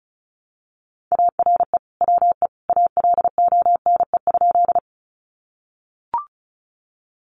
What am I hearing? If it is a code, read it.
Morse code, heard as a keyed tone, about 34 wpm, 700 Hz. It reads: ARE WE ALONE?
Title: Sentences from Top 1000 Words 35wpm